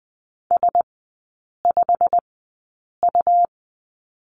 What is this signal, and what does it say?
Morse code, heard as a keyed tone, about 20 wpm, spaced as for 8 wpm, 700 Hz. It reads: S5U
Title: 100 Lesson 17e - Letter u - Three Characters 20wpm